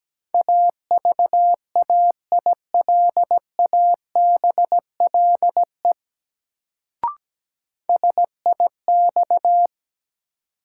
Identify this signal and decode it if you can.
Morse code, heard as a keyed tone, about 17 wpm, 700 Hz. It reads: AVAILABLE SIX
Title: Sentences from Top 600 Words - New Word Drill - 17wpm